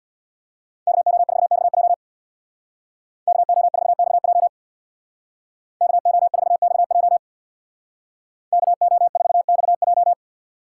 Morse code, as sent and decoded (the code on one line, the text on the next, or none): -.- --- ....- -..- .--- / -.- --- ....- -..- .--- / -.- --- ....- -..- .--- / -.- --- ....- -..- .---
KO4XJ KO4XJ KO4XJ KO4XJ